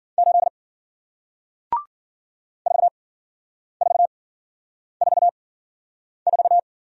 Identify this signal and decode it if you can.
Morse code, heard as a keyed tone, about 53 wpm, 700 Hz. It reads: 8 4 4 4 4